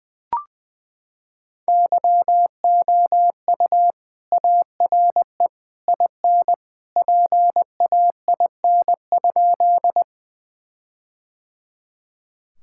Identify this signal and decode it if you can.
Morse code, heard as a keyed tone, about 20 wpm, 700 Hz. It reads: YOU ARE IN PAIN?
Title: Sentences from Top 1600 Words 20wpm